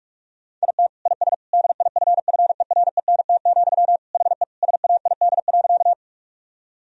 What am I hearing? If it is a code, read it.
Morse code, heard as a keyed tone, about 41 wpm, 700 Hz. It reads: IT IS DIFFERENT, HE SAID.
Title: Sentences from Top 1000 Words - Review Part 3 - 45wpm